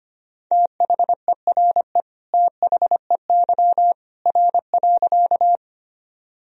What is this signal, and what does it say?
Morse code, heard as a keyed tone, about 25 wpm, 700 Hz. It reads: THERE THEY R.
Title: Sentences from Top 100 Words and Abbreviations 25wpm